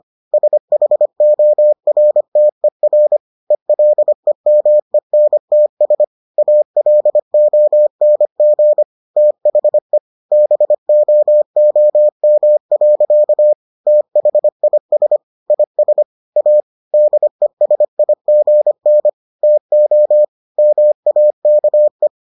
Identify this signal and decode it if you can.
Morse code, heard as a keyed tone, about 25 wpm, 600 Hz. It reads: SHORTER ELEMENTS ALONG THE BOOM. THIS IS A DESIGN TO MAKE